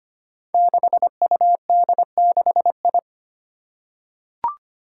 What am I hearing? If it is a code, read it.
Morse code, heard as a keyed tone, about 25 wpm, 700 Hz. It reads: 6UD6I
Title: Five Challenging Letter-Number Combinations 25wpm